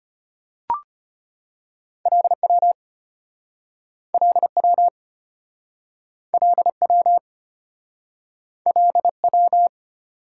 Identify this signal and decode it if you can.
Morse code, heard as a keyed tone, about 34 wpm, 700 Hz. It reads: LW LW LW LW